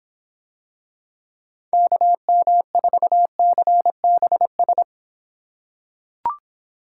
Morse code, heard as a keyed tone, about 26 wpm, 700 Hz. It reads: KM4CBS